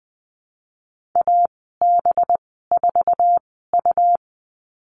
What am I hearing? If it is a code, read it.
Morse code, heard as a keyed tone, about 19 wpm, spaced as for 14 wpm, 700 Hz. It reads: AB4U
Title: 213 Lesson 34g - Number 8 - Call Signs 20wpm